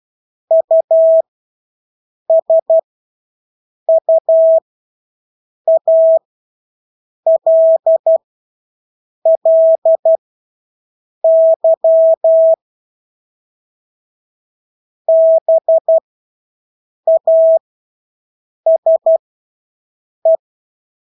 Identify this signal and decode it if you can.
Morse code, heard as a keyed tone, about 12 wpm, spaced as for 6 wpm, 650 Hz. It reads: USUALLY BASE
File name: Engelske ord 12-6 wpm